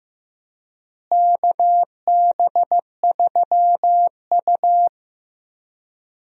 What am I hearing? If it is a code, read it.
Morse code, heard as a keyed tone, about 15 wpm, 700 Hz. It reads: KB3U
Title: USA Call Signs - 4 Characters - Repeated 2x - 15wpm